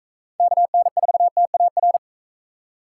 Morse code, heard as a keyed tone, about 44 wpm, 700 Hz. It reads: KN4TAR